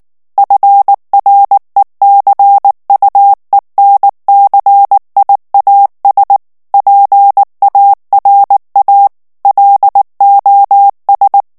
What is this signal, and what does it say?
Morse code, heard as a keyed tone, about 19 wpm, 800 Hz. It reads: FRECUENCIAS PARA LOS
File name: morse mp3 / ESPAGNOL TEXTE EN CLAIR 2 VITESSE 18 MOTS MINUTES